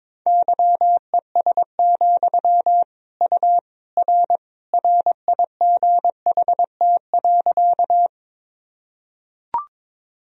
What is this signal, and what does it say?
Morse code, heard as a keyed tone, about 22 wpm, 700 Hz. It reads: YES, U R RIGHT.
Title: Sentences from Top 500 Words and Abbreviations 22wpm